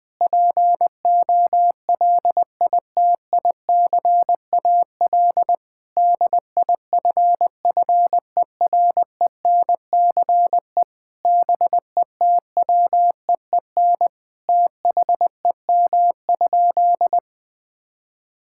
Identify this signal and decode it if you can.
Morse code, heard as a keyed tone, about 21 wpm, 700 Hz. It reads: POLITICAL DIFFERENCE BETWEEN THEM?
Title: Sentences from Top 1000 Words - Review Part 5 - 20wpm